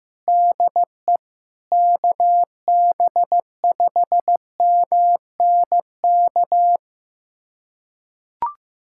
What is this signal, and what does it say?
Morse code, heard as a keyed tone, about 15 wpm, 700 Hz. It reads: DE KB5MNK
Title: USA Callsign DE Callsign - 6 Characters - 15wpm